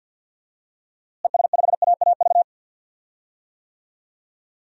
Morse code, heard as a keyed tone, about 52 wpm, 700 Hz. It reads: ES5AAV
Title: DXCC-100 Call Signs - 6 Characters - Repeated 3x - 50wpm